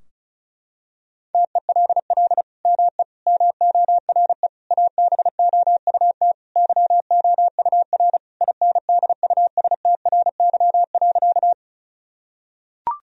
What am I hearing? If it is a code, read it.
Morse code, heard as a keyed tone, about 36 wpm, 700 Hz. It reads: TELL ME MORE ABOUT YOUR INDUSTRY.
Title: Sentences from Top 1000 Words - Review Part 4 - 35wpm